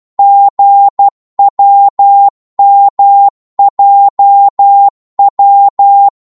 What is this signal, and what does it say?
Morse code, heard as a keyed tone, about 12 wpm, 800 Hz. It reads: GWMJW
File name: Morse / ltr-12wpm